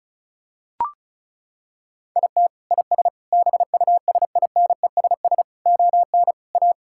Morse code, heard as a keyed tone, about 36 wpm, 700 Hz. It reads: IT IS BUSINESS ON A
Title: Sentences from Top 1000 Words - Review Part 5 - 35wpm